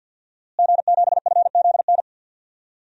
Morse code, heard as a keyed tone, about 52 wpm, 700 Hz. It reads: G7FZN
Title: DXCC-100 Call Signs - 5 Characters - Repeated 2x - 50wpm